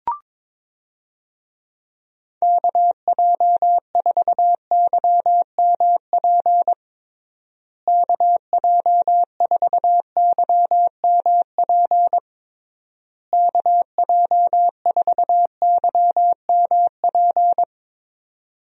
Morse code, heard as a keyed tone, about 21 wpm, 700 Hz. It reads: KJ4YMP KJ4YMP KJ4YMP
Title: USA Call Signs - 6 Characters - Repeated 3x - 22wpm